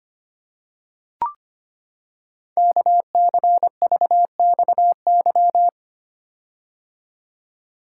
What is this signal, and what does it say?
Morse code, heard as a keyed tone, about 25 wpm, 700 Hz. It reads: KCVXY
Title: Code Groups - 5 Letters - 25wpm